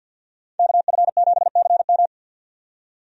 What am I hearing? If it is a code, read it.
Morse code, heard as a keyed tone, about 52 wpm, 700 Hz. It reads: KF7CM
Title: USA Call Signs - 5 Characters - 50wpm